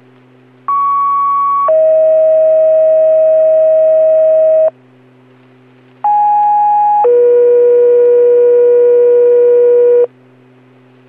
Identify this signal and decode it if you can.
Two-tone page, heard as a one-second tone, then a three-second tone, about 1120 then 640 Hz, about 830 then 480 Hz. It is heard from 0.7 to 10.1 s.